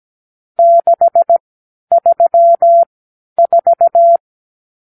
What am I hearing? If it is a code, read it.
Morse code, heard as a keyed tone, about 17 wpm, spaced as for 10 wpm, 680 Hz. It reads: 634
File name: k7qo / Lesson 50 (1-6)